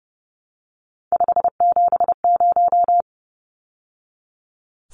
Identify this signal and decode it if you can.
Morse code, heard as a keyed tone, about 30 wpm, 700 Hz. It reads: 570
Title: Numbers - 3-Digit 30wpm